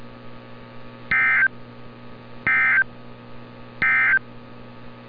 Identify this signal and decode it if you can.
Emergency Alert System (SAME) tones, heard as harsh buzzing bursts. It is heard from 1.0 to 4.3 s.
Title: NOAA Weather Radio - Altoona, PA